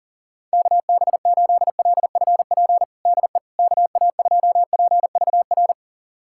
Morse code, heard as a keyed tone, about 40 wpm, 700 Hz. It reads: KB8LFP DE KA2PVR
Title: USA Callsign DE Callsign - 6 Characters - 40wpm